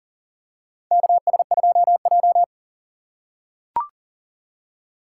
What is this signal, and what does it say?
Morse code, heard as a keyed tone, about 40 wpm, 700 Hz. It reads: KS2J
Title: USA Call Signs - 4 Characters - 3x Character Spacing - 40wpm